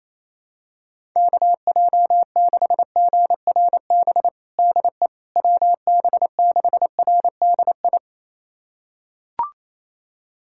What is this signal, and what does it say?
Morse code, heard as a keyed tone, about 28 wpm, 700 Hz. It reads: KJ6GRB DE WB6RDI
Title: USA Callsign DE Callsign - 6 Characters - 28wpm